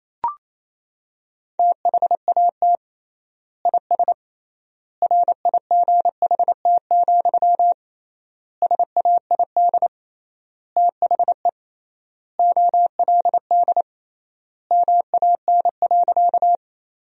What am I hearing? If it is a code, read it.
Morse code, heard as a keyed tone, about 28 wpm, 700 Hz. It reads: THAT IS RIGHT, SAID THE OLD MAN.
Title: 3x Word-Spacing - Sentences from Top 200 Words - 28wpm